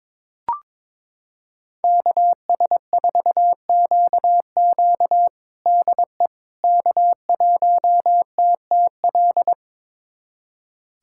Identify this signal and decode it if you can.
Morse code, heard as a keyed tone, about 21 wpm, 700 Hz. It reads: KS4QQ DE K1TTL
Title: USA Callsign DE Callsign - 5 Characters - 22wpm